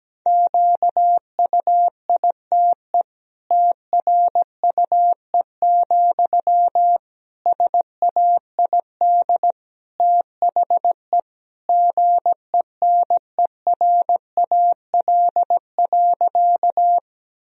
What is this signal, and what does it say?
Morse code, heard as a keyed tone, about 17 wpm, 700 Hz. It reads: QUITE TRUE, SAID THE GENERAL.
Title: Sentences from Top 1000 Words - Review Part 3 - 17wpm